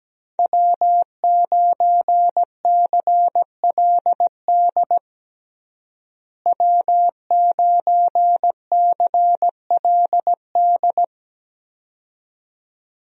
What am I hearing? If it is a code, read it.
Morse code, heard as a keyed tone, about 17 wpm, 700 Hz. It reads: W9CLD W9CLD
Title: USA Call Signs - 5 Characters - Repeated 2x - 17wpm